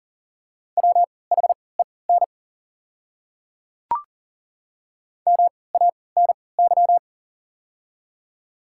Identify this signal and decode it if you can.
Morse code, heard as a keyed tone, about 40 wpm, spaced as for 23 wpm, 700 Hz. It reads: WHEN MANY
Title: 3x Character Spacing - Top 300 Words - 40wpm